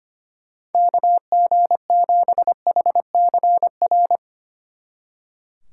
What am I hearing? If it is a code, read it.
Morse code, heard as a keyed tone, about 33 wpm, 700 Hz. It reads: KG7HCR